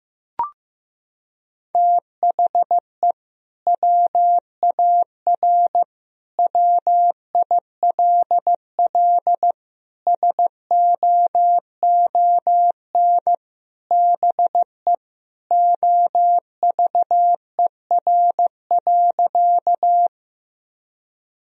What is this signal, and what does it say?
Morse code, heard as a keyed tone, about 16 wpm, 700 Hz. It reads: THE WAR WILL SOON BE OVER.
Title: Sentences from Top 1000 Words - Review Part 2 - 15wpm